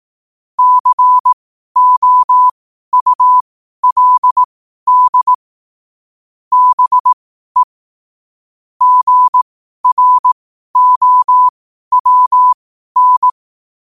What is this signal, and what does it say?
Morse code, heard as a keyed tone, about 18 wpm, spaced as for 12 wpm, 1000 Hz. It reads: COULD BE GROWN
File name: qotdfiles / 12wpm